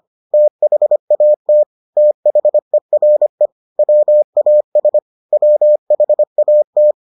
Morse code, heard as a keyed tone, about 25 wpm, 600 Hz. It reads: THAT THERE WAS WHAT